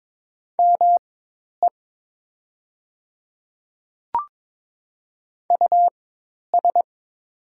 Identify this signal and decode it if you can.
Morse code, heard as a keyed tone, about 22 wpm, spaced as for 10 wpm, 700 Hz. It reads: ME US
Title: 4x Character Spacing - Top 200 Words - 22wpm